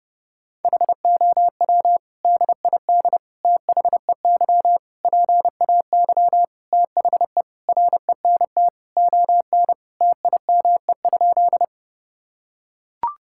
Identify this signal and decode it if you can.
Morse code, heard as a keyed tone, about 32 wpm, 700 Hz. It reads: HOW DID THEY PAY THE RENT ON TIME?
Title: Sentences from Top 1000 Words - Review Part 5 - 30wpm